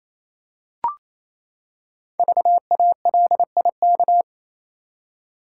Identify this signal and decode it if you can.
Morse code, heard as a keyed tone, about 27 wpm, 700 Hz. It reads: VALIK
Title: Code Groups - 5 Letters - 28wpm